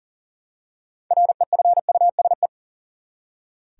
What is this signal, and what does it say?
Morse code, heard as a keyed tone, about 40 wpm, 700 Hz. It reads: REFUSE
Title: Six Letter Words - 40wpm